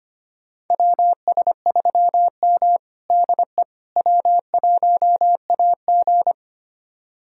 Morse code, heard as a keyed tone, about 25 wpm, 700 Hz. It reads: WS3M DE W1AG
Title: USA Callsign DE Callsign - 4 Characters - 25wpm